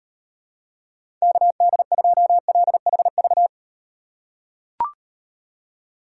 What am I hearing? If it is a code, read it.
Morse code, heard as a keyed tone, about 38 wpm, 700 Hz. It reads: KD2LHV